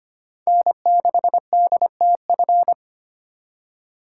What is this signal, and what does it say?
Morse code, heard as a keyed tone, about 25 wpm, 700 Hz. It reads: N6DTF
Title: USA Call Signs - 2x Character Spacing - 25wpm